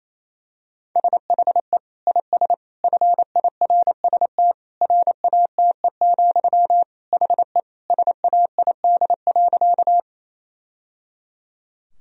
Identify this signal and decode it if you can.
Morse code, heard as a keyed tone, about 28 wpm, 700 Hz. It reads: SHE IS FIRST RATE, HE SAID.
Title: Sentences from Top 1000 Words - Review Part 4 - 28wpm